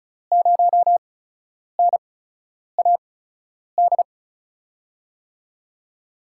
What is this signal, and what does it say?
Morse code, heard as a keyed tone, about 36 wpm, spaced as for 10 wpm, 700 Hz. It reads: 0NAD
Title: USA Call Signs - 6 Characters - 8x Character Spacing - 35wpm